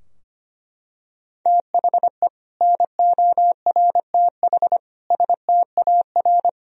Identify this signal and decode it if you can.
Morse code, heard as a keyed tone, about 25 wpm, 700 Hz. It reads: THE NORTH STAR
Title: Sentences from Top 1000 Words - Review Part 3 - 25wpm